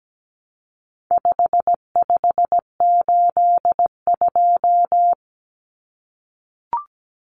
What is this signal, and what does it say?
Morse code, heard as a keyed tone, about 17 wpm, 700 Hz. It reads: 5582
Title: Numbers - 4-Digit 17wpm